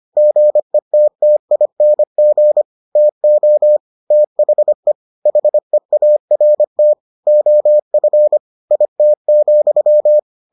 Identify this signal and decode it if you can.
Morse code, heard as a keyed tone, about 25 wpm, 600 Hz. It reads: GETTING TO THE HEART OF IT,